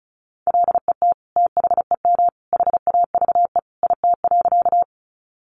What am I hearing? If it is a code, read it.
Morse code, heard as a keyed tone, about 34 wpm, 700 Hz. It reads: LET THEM HAVE IT.
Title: Sentences from Top 300 Words 35wpm